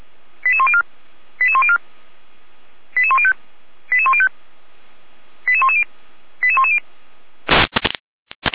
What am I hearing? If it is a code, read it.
Five-tone selective call (ZVEI), heard as quick runs of five tones, tones of about 70 ms. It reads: 88104 88104 88105 88105 88110 88110
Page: /455/SDR/ZVEI/88102/2022.09.25/